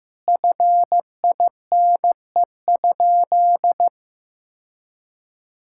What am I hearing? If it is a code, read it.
Morse code, heard as a keyed tone, about 15 wpm, 700 Hz. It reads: FINE?
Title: Sentences from Top 1000 Words - Review Part 5 - 15wpm